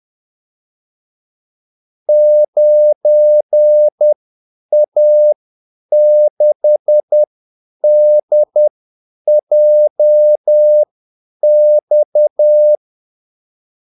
Morse code, heard as a keyed tone, about 10 wpm, spaced as for 8 wpm, 600 Hz. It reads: 9A6DJX
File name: Callsigns-10-8wpm